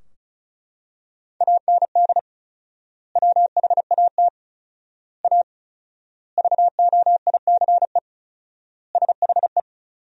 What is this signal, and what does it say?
Morse code, heard as a keyed tone, about 34 wpm, 700 Hz. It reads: AND WHAT A VOICE SHE
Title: 4x Word-Spacing - Sentences from Top 500 Words - 35wpm